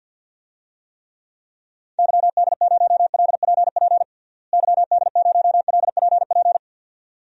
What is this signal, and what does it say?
Morse code, heard as a keyed tone, about 52 wpm, 700 Hz. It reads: YD0LPP YD0LPP
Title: DXCC-100 Call Signs - 6 Characters - Repeated 2x - 50wpm